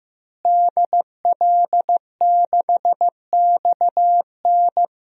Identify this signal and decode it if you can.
Morse code, heard as a keyed tone, about 15 wpm, 700 Hz. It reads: DL6XN
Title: DXCC-100 Call Signs - 5 Characters - Repeated 3x - 15wpm